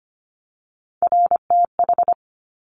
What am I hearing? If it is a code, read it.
Morse code, heard as a keyed tone, about 25 wpm, 700 Hz. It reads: RTH
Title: Top Three Letter Combinations in Words 25wpm